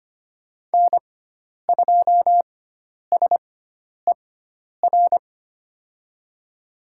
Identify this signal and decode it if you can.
Morse code, heard as a keyed tone, about 25 wpm, spaced as for 10 wpm, 700 Hz. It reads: N2SER
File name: Lesson 16g - Letter c - Call Signs 25wpm